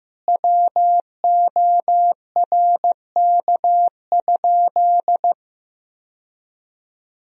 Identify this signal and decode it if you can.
Morse code, heard as a keyed tone, about 15 wpm, 700 Hz. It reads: WORK?
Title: Sentences from Top 1000 Words - Review Part 4 - 15wpm